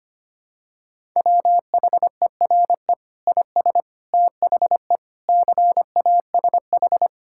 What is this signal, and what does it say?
Morse code, heard as a keyed tone, about 25 wpm, 700 Hz. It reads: WHERE IS THE CASH
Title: Sentences from Top 1000 Words - Review Part 5 - 25wpm